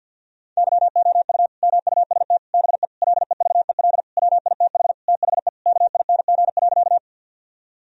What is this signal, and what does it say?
Morse code, heard as a keyed tone, about 50 wpm, 700 Hz. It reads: YOU MUST BE LEVEL WITH THE KING.
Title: Sentences from Top 1000 Words - Review Part 4 - 50wpm